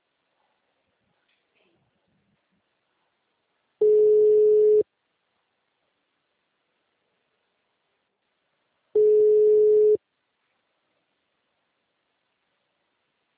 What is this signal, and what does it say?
Ringback tone (Europe, 425 Hz).
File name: out- -2017- - - .40